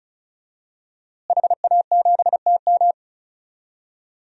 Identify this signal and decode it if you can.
Morse code, heard as a keyed tone, about 34 wpm, 700 Hz. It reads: HA7TM
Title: DXCC-100 Call Signs - 5 Characters - Repeated 2x - 35wpm